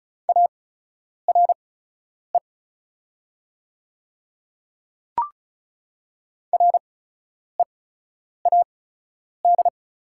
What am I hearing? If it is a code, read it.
Morse code, heard as a keyed tone, about 36 wpm, spaced as for 10 wpm, 690 Hz. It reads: ARE READ
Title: 8x Character Spacing - Top 200 Words - 35wpm